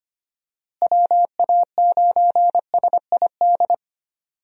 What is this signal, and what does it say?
Morse code, heard as a keyed tone, about 25 wpm, 700 Hz. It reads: WA9SID
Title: USA Call Signs - 6 Characters - Repeated 2x - 25wpm